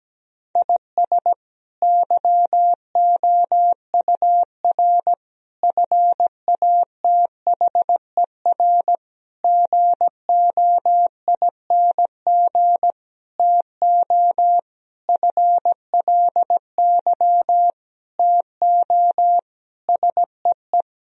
Morse code, heard as a keyed tone, about 17 wpm, 700 Hz. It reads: IS YOUR FATHER GOING TO FLY TO SEE
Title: Sentences from Top 1000 Words - Review Part 2 - 17wpm